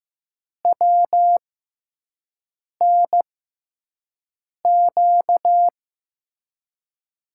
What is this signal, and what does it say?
Morse code, heard as a keyed tone, about 14 wpm, spaced as for 5 wpm, 700 Hz. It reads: WNQ
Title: USA Call Signs - 6 Characters - 6x Character Spacing - 15wpm